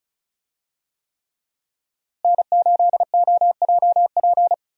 Morse code, heard as a keyed tone, about 36 wpm, 700 Hz. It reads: N8OJP
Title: USA Call Signs - 5 Characters - Repeated 3x - 35wpm